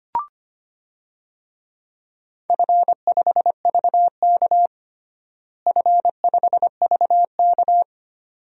Morse code, heard as a keyed tone, about 25 wpm, 700 Hz. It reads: F5VK F5VK
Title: DXCC-100 Call Signs - 4 Characters - Repeated 2x - 25wpm